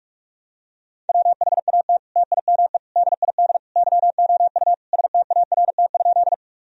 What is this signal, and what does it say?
Morse code, heard as a keyed tone, about 46 wpm, 700 Hz. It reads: WHAT TIME DID YOU START?
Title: Sentences from Top 1000 Words - Review Part 1 - 45wpm